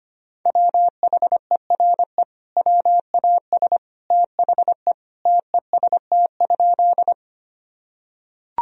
Morse code, heard as a keyed tone, about 25 wpm, 700 Hz. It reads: WHERE WAS THE TEST?
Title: Sentences from Top 1000 Words - Review Part 3 - 25wpm